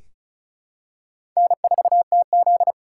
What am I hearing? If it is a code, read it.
Morse code, heard as a keyed tone, about 34 wpm, 700 Hz. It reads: N4TZ
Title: K1USN SST QSO Practice - 35wpm